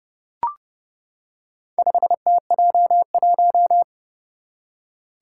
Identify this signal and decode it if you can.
Morse code, heard as a keyed tone, about 30 wpm, 700 Hz. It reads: 5TJ1
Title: Four Challenging Letter-Number Combinations 30wpm